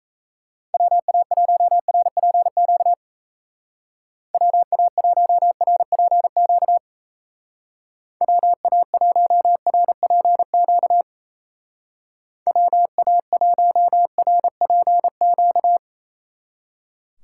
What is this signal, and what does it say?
Morse code, heard as a keyed tone, about 35 wpm, 700 Hz. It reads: WA1RPQ WA1RPQ WA1RPQ WA1RPQ
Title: Speed Racing - USA 2x3 Call Signs 28wpm